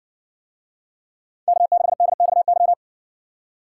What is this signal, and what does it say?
Morse code, heard as a keyed tone, about 60 wpm, 700 Hz. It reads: K6DXY